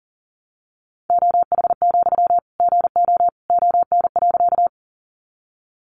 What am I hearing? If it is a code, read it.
Morse code, heard as a keyed tone, about 40 wpm, 700 Hz. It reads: OH, GO ON.
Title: Sentences from Top 500 Words 40wpm